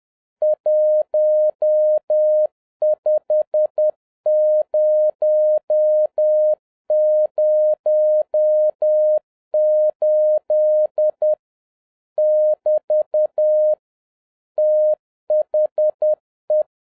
Morse code, heard as a keyed tone, about 10 wpm, 610 Hz. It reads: 15008 <BT> THE